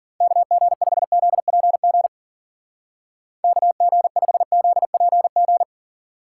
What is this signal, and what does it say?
Morse code, heard as a keyed tone, about 50 wpm, 700 Hz. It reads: KG5ZPG KG5ZPG
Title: Speed Racing - USA 2x3 Call Signs 40wpm